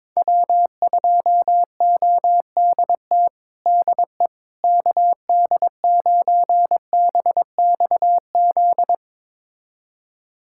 Morse code, heard as a keyed tone, about 22 wpm, 700 Hz. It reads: W2ODT DE KD9BXZ
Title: USA Callsign DE Callsign 22wpm